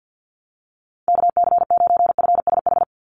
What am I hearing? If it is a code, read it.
Morse code, heard as a keyed tone, about 50 wpm, 700 Hz. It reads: KC9FSH
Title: USA Call Signs 50wpm